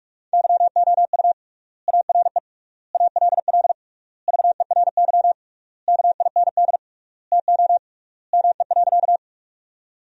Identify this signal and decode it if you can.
Morse code, heard as a keyed tone, about 41 wpm, 700 Hz. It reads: YOU ARE ALL VERY KIND TO ME.
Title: 3x Word-Spacing - Sentences from Top 200 Words - 45wpm